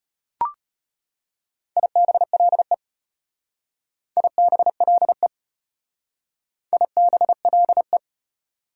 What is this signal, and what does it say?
Morse code, heard as a keyed tone, about 35 wpm, 700 Hz. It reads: IBLE IBLE IBLE